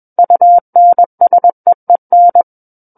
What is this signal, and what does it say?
Morse code, heard as a keyed tone, about 21 wpm, 700 Hz. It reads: UNSEEN
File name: Book One / 12-Chapter 12 - 21 WPM